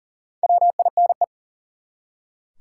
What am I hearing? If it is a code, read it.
Morse code, heard as a keyed tone, about 40 wpm, 700 Hz. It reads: WINE
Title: Four Letter Words - 40wpm